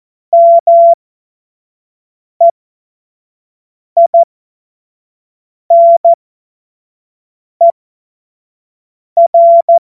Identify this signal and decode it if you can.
Morse code, heard as a keyed tone, about 14 wpm, spaced as for 5 wpm, 680 Hz. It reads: MEINER
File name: Morsekurs / Witze / Witz 04. Januar 2022, 25 F70